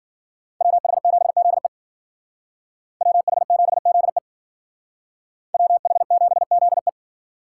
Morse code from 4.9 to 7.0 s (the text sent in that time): WH7ZE